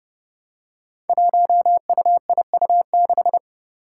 Morse code, heard as a keyed tone, about 30 wpm, 700 Hz. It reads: 1UIU6
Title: Five Challenging Letter-Number Combinations 30wpm